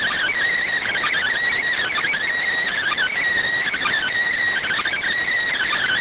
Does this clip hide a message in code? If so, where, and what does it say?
Yes: Emergency Alert System (SAME) tones (header bursts) from 0.8 to 5.0 s, heard as harsh buzzing bursts.